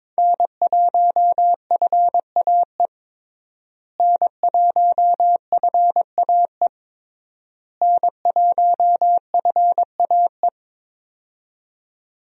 Morse code, heard as a keyed tone, about 22 wpm, 700 Hz. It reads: N1FAE N1FAE N1FAE
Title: USA Call Signs - 5 Characters - Repeated 3x - 22wpm